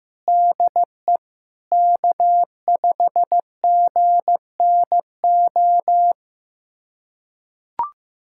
Morse code, heard as a keyed tone, about 15 wpm, 700 Hz. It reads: DE K5GNO
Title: USA Callsign DE Callsign - 5 Characters - 15wpm